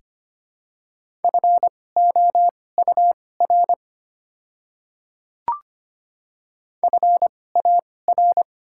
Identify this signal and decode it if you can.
Morse code, heard as a keyed tone, about 25 wpm, spaced as for 18 wpm, 700 Hz. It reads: FOUR FAR
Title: 2x Character Spacing - Top 300 Words - 25wpm